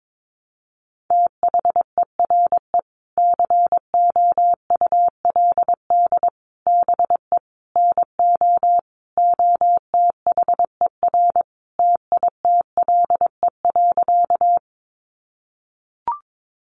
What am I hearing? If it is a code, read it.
Morse code, heard as a keyed tone, about 22 wpm, 700 Hz. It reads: THERE COULD BE NO OTHER TITLE.
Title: Sentences from Top 1200 Words 22wpm